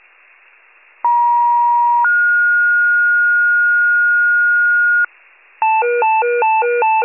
Two-tone page, heard as a one-second tone, then a three-second tone, about 950 then 1430 Hz, with noise. It is heard from 1.0 to 5.0 s.
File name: wcalert